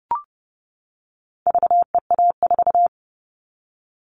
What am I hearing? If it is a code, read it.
Morse code, heard as a keyed tone, about 30 wpm, 700 Hz. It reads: VEA4
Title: All Cut Numbers - 4-Digit 30wpm